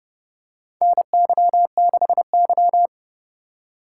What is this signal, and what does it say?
Morse code, heard as a keyed tone, about 30 wpm, 700 Hz. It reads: NY6Y
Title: K1USN SST QSO Practice - 30wpm